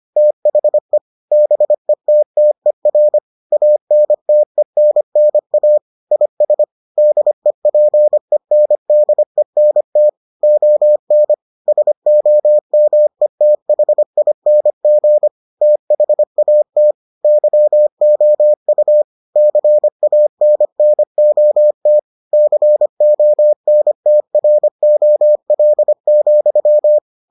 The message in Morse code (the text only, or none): THE BETTER ANTENNA IS DEPENDENT ON SOMETHING THAT YOU CANNOT CONTROL,